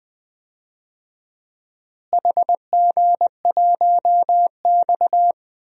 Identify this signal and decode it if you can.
Morse code, heard as a keyed tone, about 20 wpm, 700 Hz. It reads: HG1X
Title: DXCC-100 Call Signs - 4 Characters - Repeated 2x - 20wpm